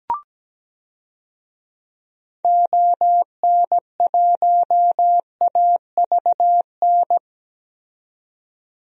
Morse code, heard as a keyed tone, about 17 wpm, 700 Hz. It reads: ON1AVN
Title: DXCC-100 Call Signs - 6 Characters - 17wpm